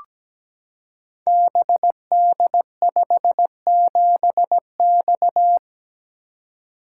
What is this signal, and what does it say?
Morse code, heard as a keyed tone, about 17 wpm, 700 Hz. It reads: BD57X
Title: Code Groups - 5 Letter:Numbers - 17wpm